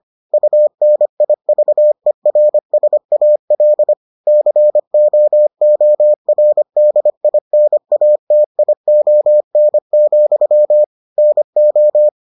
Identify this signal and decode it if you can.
Morse code, heard as a keyed tone, about 25 wpm, 600 Hz. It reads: UNIVERSAL COORDINATION, NO